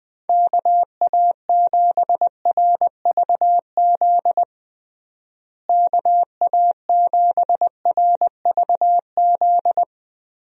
Morse code, heard as a keyed tone, about 20 wpm, 700 Hz. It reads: KA7RVZ KA7RVZ
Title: USA Call Signs - 6 Characters - Repeated 2x - 20wpm